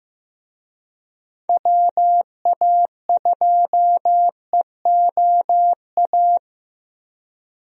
Morse code, heard as a keyed tone, about 16 wpm, 700 Hz. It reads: WA2EOA
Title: USA Call Signs - 6 Characters - Repeated 3x - 15wpm